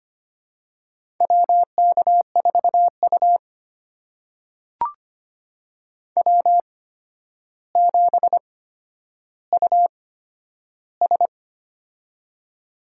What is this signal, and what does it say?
Morse code, heard as a keyed tone, about 25 wpm, spaced as for 11 wpm, 700 Hz. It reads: WK4U W7US
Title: USA Call Signs - 4 Characters - 8x Character Spacing - 25wpm